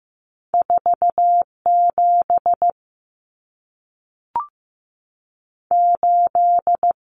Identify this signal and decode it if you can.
Morse code, heard as a keyed tone, about 15 wpm, 700 Hz. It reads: 4 7 8
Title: Common Cut Numbers - 2-Digit 15wpm